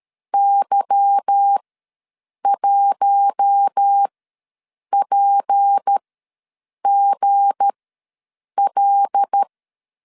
Morse code, heard as a keyed tone, about 13 wpm, spaced as for 7 wpm, 800 Hz. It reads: Y1PGL
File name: LEZIONI / 8 YJ45 / 8 Y ws 12wpm 3409